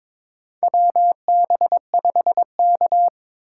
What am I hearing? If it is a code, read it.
Morse code, heard as a keyed tone, about 22 wpm, 700 Hz. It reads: WB5K
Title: K1USN SST QSO Practice - 22wpm